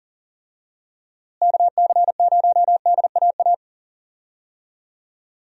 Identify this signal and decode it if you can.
Morse code, heard as a keyed tone, about 40 wpm, 700 Hz. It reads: KC0DAA
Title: USA Call Signs - 6 Characters - Repeated 3x - 40wpm